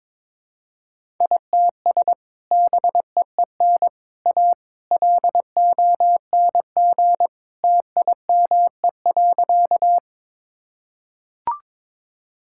Morse code, heard as a keyed tone, about 22 wpm, 700 Hz. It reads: ITS BEEN A LONG TIME.
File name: practice / 22wpm / Sentences with 3-Word Phrases - 22wpm